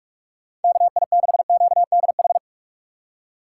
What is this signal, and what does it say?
Morse code, heard as a keyed tone, about 41 wpm, 700 Hz. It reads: KI6QDH
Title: K1USN SST QSO Practice - 45wpm